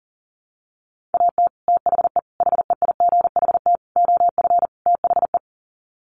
Morse code, heard as a keyed tone, about 40 wpm, 700 Hz. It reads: AT THE HEIGHT OF THE
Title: Sets of 5 Words 40wpm